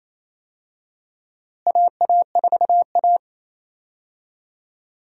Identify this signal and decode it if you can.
Morse code, heard as a keyed tone, about 28 wpm, 700 Hz. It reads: AA4A
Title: USA Call Signs - 4 Characters - Repeated 2x - 28wpm